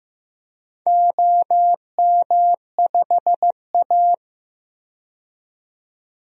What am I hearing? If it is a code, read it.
Morse code, heard as a keyed tone, about 15 wpm, 700 Hz. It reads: OM5A
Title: DXCC-100 Call Signs - 4 Characters - 15wpm